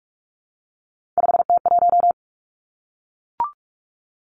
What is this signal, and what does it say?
Morse code, heard as a keyed tone, about 46 wpm, 700 Hz. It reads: 5T1
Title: Three Challenging Letter-Number Combinations 45wpm